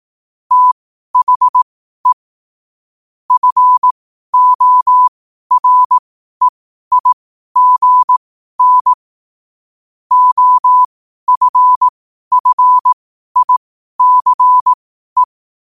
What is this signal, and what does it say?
Morse code, heard as a keyed tone, about 18 wpm, spaced as for 12 wpm, 1000 Hz. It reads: THE FOREIGN OFFICE
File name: qotdfiles / 12wpm